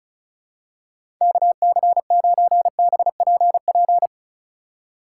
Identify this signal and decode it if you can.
Morse code, heard as a keyed tone, about 36 wpm, 700 Hz. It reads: KC9BPP
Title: USA Call Signs - 6 Characters - 4x Character Spacing - 35wpm